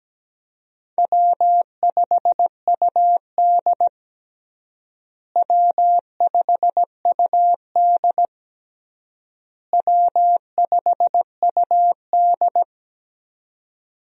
Morse code, heard as a keyed tone, about 17 wpm, 700 Hz. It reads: W5UD W5UD W5UD
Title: USA Call Signs - 4 Characters - Repeated 3x - 17wpm